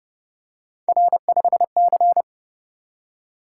Morse code, heard as a keyed tone, about 29 wpm, 700 Hz. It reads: R5C